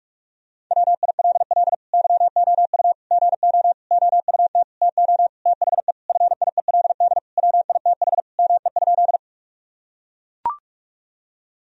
Morse code, heard as a keyed tone, about 41 wpm, 700 Hz. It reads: WILL YOU GO OUT TO THE FIELD WITH ME?
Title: Sentences from Top 1000 Words - Review Part 2 - 45wpm